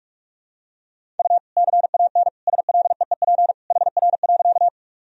Morse code, heard as a keyed tone, about 46 wpm, 700 Hz. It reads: U CAN SLEEP HR.
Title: Sentences from Top 500 Words and Abbreviations 45wpm